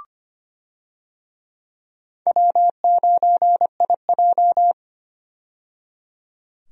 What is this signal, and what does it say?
Morse code, heard as a keyed tone, about 25 wpm, 700 Hz. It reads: W9IJ
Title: USA Call Signs - 4 Characters - 25wpm